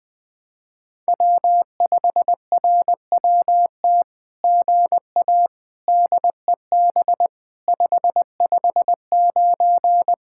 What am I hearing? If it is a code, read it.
Morse code, heard as a keyed tone, about 20 wpm, 700 Hz. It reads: W5RWT GA DEB 559
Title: SOTA QSO Practice 20wpm